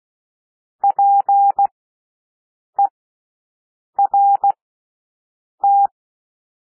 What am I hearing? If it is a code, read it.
Morse code, heard as a keyed tone, about 16 wpm, spaced as for 6 wpm, 800 Hz. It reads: PERT
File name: cwmp3 / W1 SOUND WORDS 16 wpm